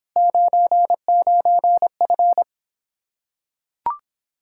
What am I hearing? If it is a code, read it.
Morse code, heard as a keyed tone, about 26 wpm, 700 Hz. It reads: 99F